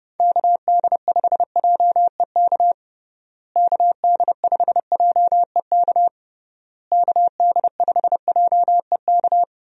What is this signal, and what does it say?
Morse code, heard as a keyed tone, about 30 wpm, 700 Hz. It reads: KD5JEK KD5JEK KD5JEK
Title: USA Call Signs - 6 Characters - Repeated 3x - 30wpm